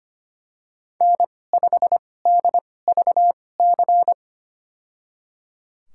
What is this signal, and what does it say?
Morse code, heard as a keyed tone, about 25 wpm, spaced as for 18 wpm, 700 Hz. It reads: N5DVC
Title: USA Call Signs - 2x Character Spacing - 25wpm